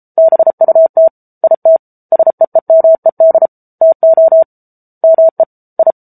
Morse code, heard as a keyed tone, about 34 wpm, 650 Hz. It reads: BUT IT SEEMED TO ME I
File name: Book Two / 07-Chapter 7 - 33 WPM